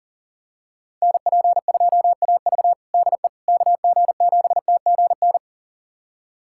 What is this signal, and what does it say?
Morse code, heard as a keyed tone, about 37 wpm, 700 Hz. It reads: NP2AV DE KG7TGN
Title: USA Callsign DE Callsign 40wpm